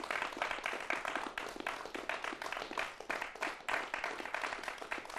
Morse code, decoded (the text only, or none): none